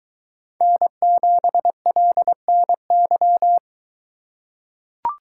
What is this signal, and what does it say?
Morse code, heard as a keyed tone, about 22 wpm, 700 Hz. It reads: N7LNY